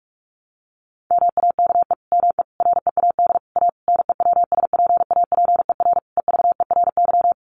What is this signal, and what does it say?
Morse code, heard as a keyed tone, about 44 wpm, 700 Hz. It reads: MAKE ME READ A NEWSPAPER EVERY
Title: Sentences from Top 1100 Words 45wpm